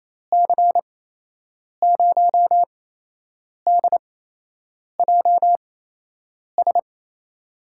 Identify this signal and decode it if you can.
Morse code, heard as a keyed tone, about 28 wpm, spaced as for 8 wpm, 700 Hz. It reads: C0DJS
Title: USA Call Signs - 8x Character Spacing - 28wpm